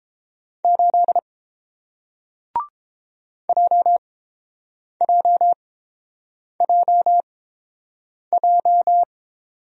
Morse code, heard as a keyed tone, about 30 wpm, spaced as for 7 wpm, 700 Hz. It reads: 8 JJJJ